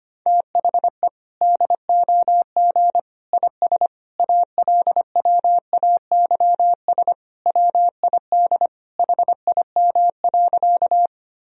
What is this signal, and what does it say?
Morse code, heard as a keyed tone, about 25 wpm, 700 Hz. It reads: THE DOG IS ALWAYS WID HIM.
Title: Sentences from Top 400 Words and Abbreviations 25wpm